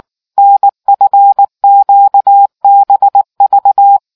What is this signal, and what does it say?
Morse code, heard as a keyed tone, about 19 wpm, 800 Hz. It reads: NFQBV